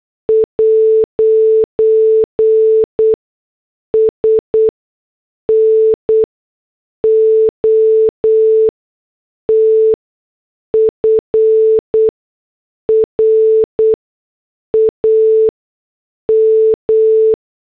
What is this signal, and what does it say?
Morse code, heard as a keyed tone, about 8 wpm, spaced as for 6 wpm, 440 Hz. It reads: 'S NOT FRAM
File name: morse / headlines